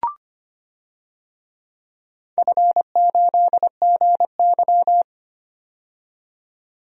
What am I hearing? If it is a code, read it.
Morse code, heard as a keyed tone, about 25 wpm, 700 Hz. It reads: F8GY
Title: DXCC-100 Call Signs - 4 Characters - 25wpm